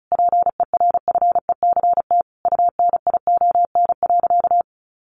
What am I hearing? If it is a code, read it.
Morse code, heard as a keyed tone, about 34 wpm, 700 Hz. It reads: PERFECT UNION.
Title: Sentences from Top 1400 Words 35wpm